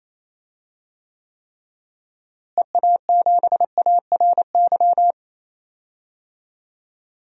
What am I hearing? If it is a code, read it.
Morse code, heard as a keyed tone, about 28 wpm, 700 Hz. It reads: EA7ARY
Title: DXCC-100 Call Signs - 6 Characters - Repeated 3x - 28wpm